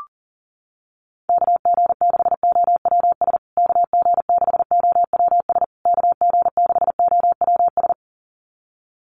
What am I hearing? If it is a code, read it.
Morse code, heard as a keyed tone, about 40 wpm, 700 Hz. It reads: KG6OWS KG6OWS KG6OWS
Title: USA Call Signs Repeated 3x 40wpm